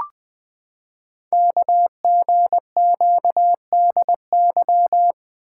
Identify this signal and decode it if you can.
Morse code, heard as a keyed tone, about 20 wpm, 700 Hz. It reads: KGQDY
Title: Code Groups - 5 Letters - 20wpm